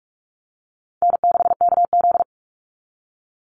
Morse code, heard as a keyed tone, about 44 wpm, 700 Hz. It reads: N6KZ
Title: NA SOTA Chaser Callsigns 45wpm